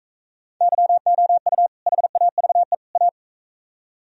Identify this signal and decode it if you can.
Morse code, heard as a keyed tone, about 42 wpm, 700 Hz. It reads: YOU HAVE A